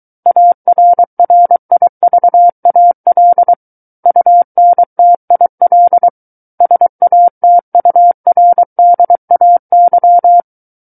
Morse code, heard as a keyed tone, about 23 wpm, 700 Hz. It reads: ARRIVAL UNTIL SATURDAY
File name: Book One / 14-Chapter 14 - 23 WPM